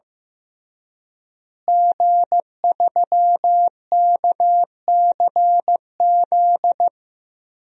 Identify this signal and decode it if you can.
Morse code, heard as a keyed tone, about 15 wpm, 700 Hz. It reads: G3KCZ
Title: DXCC-100 Call Signs - 5 Characters - Repeated 2x - 15wpm